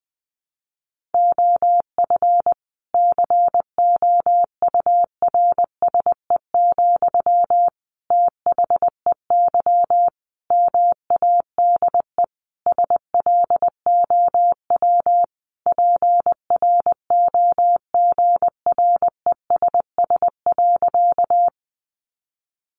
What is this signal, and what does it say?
Morse code, heard as a keyed tone, about 20 wpm, 700 Hz. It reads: OF COURSE, THEY MADE SLOW PROGRESS.
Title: Sentences from Top 1400 Words 20wpm